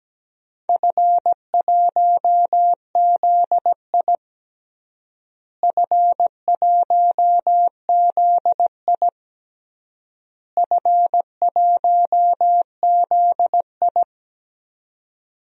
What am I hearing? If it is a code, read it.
Morse code, heard as a keyed tone, about 17 wpm, 700 Hz. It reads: F1ZI F1ZI F1ZI
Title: DXCC-100 Call Signs - 4 Characters - Repeated 3x - 17wpm